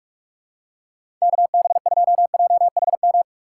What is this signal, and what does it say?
Morse code, heard as a keyed tone, about 44 wpm, 700 Hz. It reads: KB2JHM
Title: USA Call Signs - 6 Characters - 6x Character Spacing - 45wpm